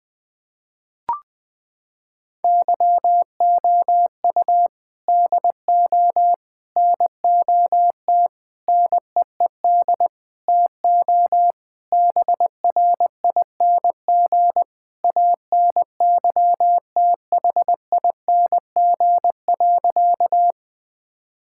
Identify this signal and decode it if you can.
Morse code, heard as a keyed tone, about 21 wpm, 700 Hz. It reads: YOU DO NOT NEED TO BRING ANYTHING.
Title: Sentences with 4-Word Phrases - 20wpm